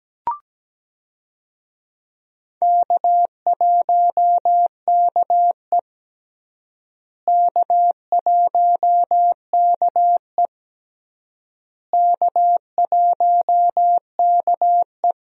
Morse code, heard as a keyed tone, about 17 wpm, 700 Hz. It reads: K1KE K1KE K1KE
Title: USA Call Signs - 4 Characters - Repeated 3x - 17wpm